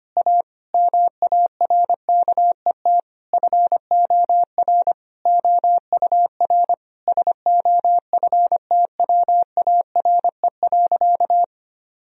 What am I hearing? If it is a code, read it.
Morse code, heard as a keyed tone, about 25 wpm, 700 Hz. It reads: A MARKET FOR OUR SOFTWARE.
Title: Sentences from Top 1000 Words - Review Part 3 - 25wpm